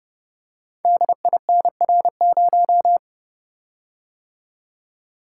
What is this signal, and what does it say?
Morse code, heard as a keyed tone, about 30 wpm, 700 Hz. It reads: DINR0
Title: Code Groups - 5 Letter:Numbers - 30wpm